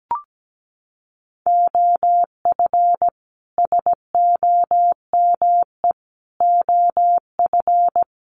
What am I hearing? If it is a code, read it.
Morse code, heard as a keyed tone, about 17 wpm, 700 Hz. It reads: OF SOME OF
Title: Sets of 4 Words 17wpm